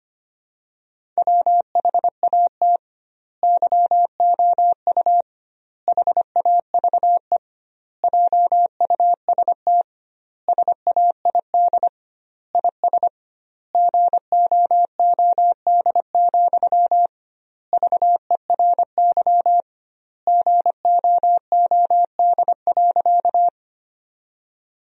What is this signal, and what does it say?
Morse code, heard as a keyed tone, about 25 wpm, 700 Hz. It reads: WHAT YOU HAVE JUST SAID IS GOOD, VERY GOOD.
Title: 2x Word-Spacing - Sentences from Top 200 Words - 25wpm